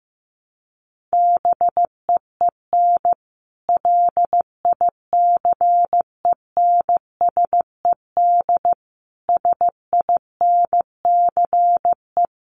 Morse code, heard as a keyed tone, about 15 wpm, 700 Hz. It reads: BEEN LICENSED SINCE
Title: Common QSO Phrases 15wpm